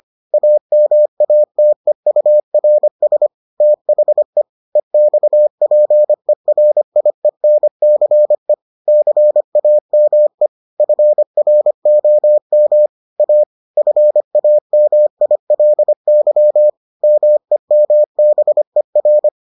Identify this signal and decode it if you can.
Morse code, heard as a keyed tone, about 25 wpm, 600 Hz. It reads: AMATEURS THE EXPERIENCE CAME FROM A FAMILY MEMBER